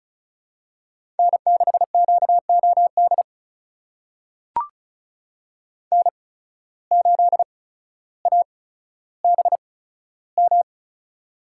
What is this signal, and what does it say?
Morse code, heard as a keyed tone, about 36 wpm, spaced as for 15 wpm, 700 Hz. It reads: N6QOD N8ABM
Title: USA Call Signs - 5 Characters - 8x Character Spacing - 35wpm